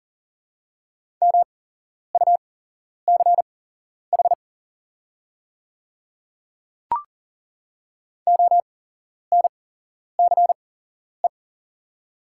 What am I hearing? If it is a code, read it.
Morse code, heard as a keyed tone, about 40 wpm, spaced as for 11 wpm, 700 Hz. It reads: MUCH ONCE
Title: 8x Character Spacing - Top 300 Words - 40wpm